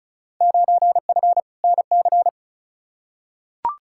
Morse code, heard as a keyed tone, about 34 wpm, 700 Hz. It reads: 9F NC
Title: Field Day Exchanges Encore - 35wpm